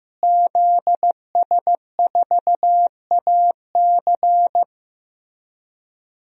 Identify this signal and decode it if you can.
Morse code, heard as a keyed tone, about 15 wpm, 700 Hz. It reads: ZS4AC
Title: DXCC-100 Call Signs - 5 Characters - Repeated 2x - 15wpm